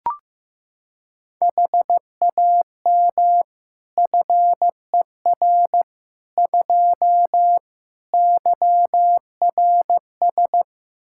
Morse code, heard as a keyed tone, about 15 wpm, 700 Hz. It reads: HAM FER 2 YRS
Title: Practice QSO - Exchange 3 - More Info - 15wpm